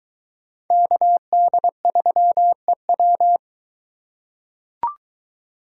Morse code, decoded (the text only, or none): KD3EW